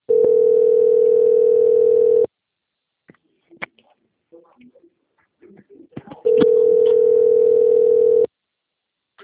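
Ringback tone (North America), 440 + 480 Hz.
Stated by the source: Outgoing telephone call from the Atibaia - SP 1 (CP Cartao) line.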